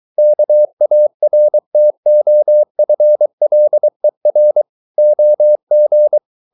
Morse code, heard as a keyed tone, about 23 wpm, 600 Hz. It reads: KARTOFLER OG
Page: Ferie 23 wpm